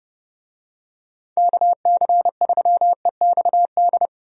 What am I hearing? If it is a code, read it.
Morse code, heard as a keyed tone, about 30 wpm, 700 Hz. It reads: KC3EXD
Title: USA Call Signs - 6 Characters - 30wpm